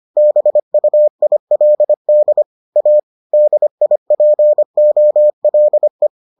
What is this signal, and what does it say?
Morse code, heard as a keyed tone, about 25 wpm, 600 Hz. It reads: BUILD A DIPOLE